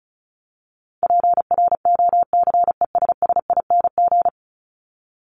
Morse code, heard as a keyed tone, about 34 wpm, 700 Hz. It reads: PROCESSING
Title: ING Suffix - Top 100 Words with Base Words 35wpm